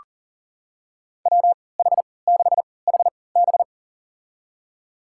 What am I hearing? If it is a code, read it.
Morse code, heard as a keyed tone, about 40 wpm, spaced as for 23 wpm, 700 Hz. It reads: WH6HB
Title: USA Call Signs - 5 Characters - 3x Character Spacing - 40wpm